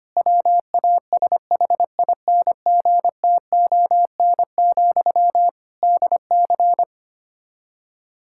Morse code, heard as a keyed tone, about 25 wpm, 700 Hz. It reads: WASHINGTON, DC
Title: Practice QSO - Exchange 1 - Beginning - 25wpm